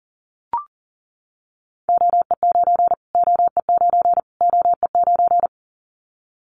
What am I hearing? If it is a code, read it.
Morse code, heard as a keyed tone, about 40 wpm, 700 Hz. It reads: OE9 OE9 OE9
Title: Callsign Prefixes for Top 25 DXCC - Repeated 3x - 40wpm